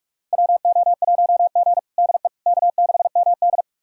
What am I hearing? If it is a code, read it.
Morse code, heard as a keyed tone, about 46 wpm, 700 Hz. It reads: WO1G DE K6MD
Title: USA Callsign DE Callsign - 4 Characters - 45wpm